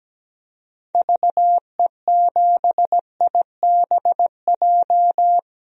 Morse code, heard as a keyed tone, about 17 wpm, 700 Hz. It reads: VE7IBJ
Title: DXCC-100 Call Signs - 6 Characters - Repeated 3x - 17wpm